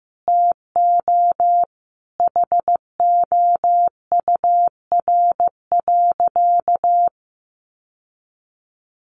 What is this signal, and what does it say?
Morse code, heard as a keyed tone, about 15 wpm, 700 Hz. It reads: TO HOUR.
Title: Sentences from Top 1200 Words 15wpm